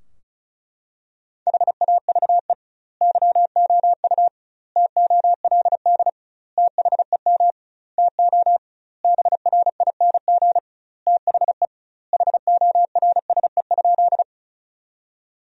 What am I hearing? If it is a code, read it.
Morse code, heard as a keyed tone, about 34 wpm, 700 Hz. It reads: HAVE YOU TOLD THEM TO BRING THE HORSE?
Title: 2x Word-Spacing - Sentences from Top 500 Words - 35wpm